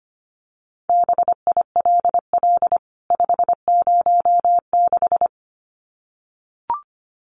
Morse code, heard as a keyed tone, about 25 wpm, 700 Hz. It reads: BILL 506
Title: CWT Exchange 25wpm